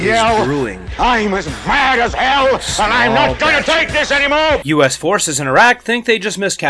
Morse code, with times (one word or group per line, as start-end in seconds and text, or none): none